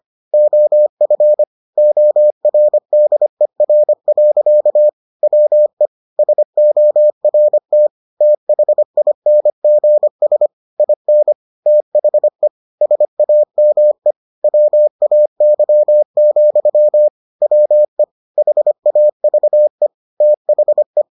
Morse code, heard as a keyed tone, about 25 wpm, 600 Hz. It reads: OF ORDER. WE SORT THINGS IN THE SAME WAY, WE HAVE THE